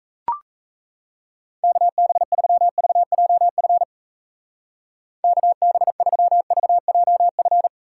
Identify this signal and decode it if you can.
Morse code, heard as a keyed tone, about 35 wpm, 700 Hz. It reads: KB3VJF KB3VJF
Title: Speed Racing - USA 2x3 Call Signs 28wpm